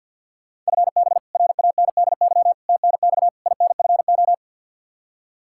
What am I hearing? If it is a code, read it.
Morse code, heard as a keyed tone, about 48 wpm, 700 Hz. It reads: FB RANDY TNX INFO
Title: K1USN SST QSO Practice 50wpm